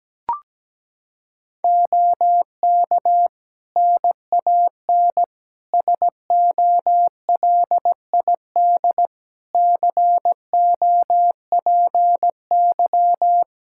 Morse code, heard as a keyed tone, about 17 wpm, 700 Hz. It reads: OK NAN SOLID COPY
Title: Practice QSO - Exchange 3 - More Info - 17wpm